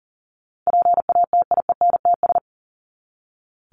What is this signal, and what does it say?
Morse code, heard as a keyed tone, about 40 wpm, 700 Hz. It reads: PATIENTS
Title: ES:S Suffix - Top 100 Words with Base Words 40wpm